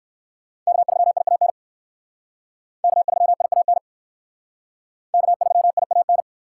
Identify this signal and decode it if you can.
Morse code, heard as a keyed tone, about 57 wpm, 690 Hz. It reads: K3IAN K3IAN K3IAN